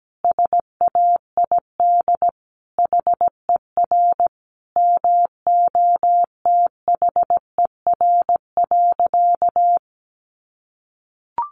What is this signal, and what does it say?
Morse code, heard as a keyed tone, about 17 wpm, 700 Hz. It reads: SAID HER MOTHER.
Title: Sentences from Top 700 Words 17wpm